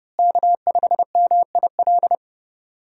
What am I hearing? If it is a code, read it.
Morse code, heard as a keyed tone, about 30 wpm, 700 Hz. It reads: K5MIL
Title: USA Call Signs - 5 Characters - Repeated 2x - 30wpm